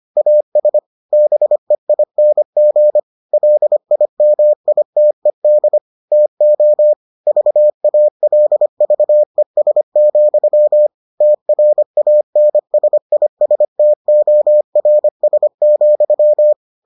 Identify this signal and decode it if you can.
Morse code, heard as a keyed tone, about 25 wpm, 600 Hz. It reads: AS BEING LIMITED TO VALVES, TRANSISTORS,